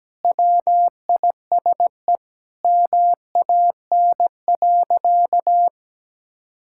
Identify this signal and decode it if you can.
Morse code, heard as a keyed tone, about 17 wpm, 700 Hz. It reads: WISE MAN.
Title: Sentences from Top 1600 Words 17wpm